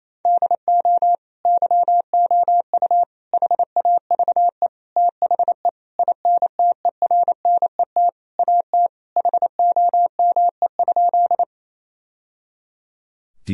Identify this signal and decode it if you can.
Morse code, heard as a keyed tone, about 28 wpm, 700 Hz. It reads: DO YOU HAVE THE INTERNET AT HOME?
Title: Sentences from Top 1000 Words - Review Part 4 - 28wpm